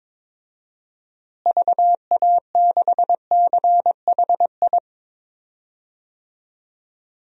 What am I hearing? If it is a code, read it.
Morse code, heard as a keyed tone, about 22 wpm, 700 Hz. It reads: VA6CHI
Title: DXCC-100 Call Signs - 6 Characters - Repeated 2x - 22wpm